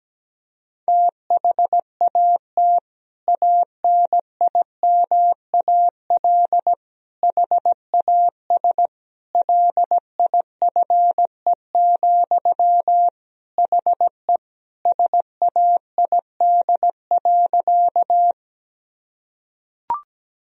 Morse code, heard as a keyed tone, about 17 wpm, 700 Hz. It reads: THAT ANIMAL HAS LIFE, HE SAID.
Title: Sentences from Top 1000 Words - Review Part 1 - 17wpm